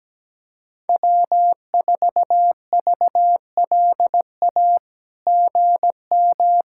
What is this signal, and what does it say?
Morse code, heard as a keyed tone, about 17 wpm, 700 Hz. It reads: W4VLA GM
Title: SOTA QSO Practice 17wpm